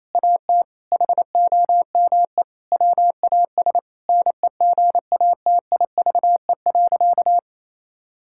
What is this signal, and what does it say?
Morse code, heard as a keyed tone, about 28 wpm, 700 Hz. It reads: AT HOME WAS NEGATIVE.
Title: Sentences from Top 1000 Words - Review Part 6 - 28wpm